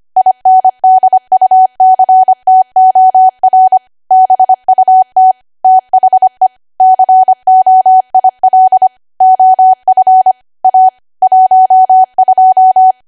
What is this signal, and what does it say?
Morse code, heard as a keyed tone, about 25 wpm, 750 Hz. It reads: INDUCTOR BUT THE COIL OF A 12